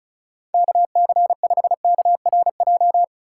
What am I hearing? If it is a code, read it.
Morse code, heard as a keyed tone, about 34 wpm, 700 Hz. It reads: KC5KRJ
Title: USA Call Signs - 6 Characters - 4x Character Spacing - 35wpm